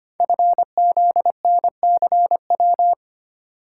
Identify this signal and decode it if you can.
Morse code, heard as a keyed tone, about 25 wpm, 700 Hz. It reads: FZNCW
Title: Code Groups - 5 Letters - 25wpm